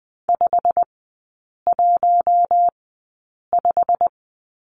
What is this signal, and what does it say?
Morse code, heard as a keyed tone, about 20 wpm, spaced as for 8 wpm, 690 Hz. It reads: 515
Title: 087 Lesson 15e - Number 5 - Three Characters 20wpm